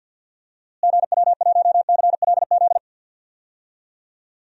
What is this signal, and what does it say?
Morse code, heard as a keyed tone, about 52 wpm, 700 Hz. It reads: GW1CLZ
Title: DXCC-100 Call Signs - 6 Characters - Repeated 3x - 50wpm